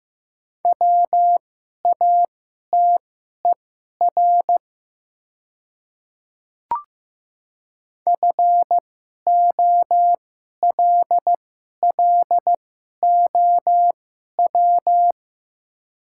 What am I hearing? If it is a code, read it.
Morse code, heard as a keyed tone, about 15 wpm, spaced as for 11 wpm, 700 Hz. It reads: WATER FOLLOW
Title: 2x Character Spacing - Top 200 Words - 15wpm